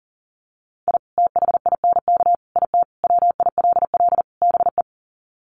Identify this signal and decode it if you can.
Morse code, heard as a keyed tone, about 40 wpm, 700 Hz. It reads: I THINK IT WILL BE
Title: Sets of 5 Words Encore2 40wpm